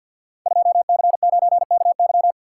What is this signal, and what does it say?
Morse code, heard as a keyed tone, about 52 wpm, 700 Hz. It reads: 2C9KY
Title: Code Groups - 5 Letter:Numbers - 50wpm